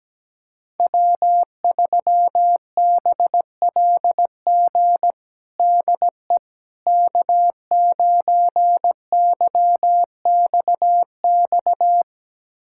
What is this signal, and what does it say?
Morse code, heard as a keyed tone, about 17 wpm, 700 Hz. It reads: W3BLG DE K9YXX
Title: USA Callsign DE Callsign - 5 Characters - 17wpm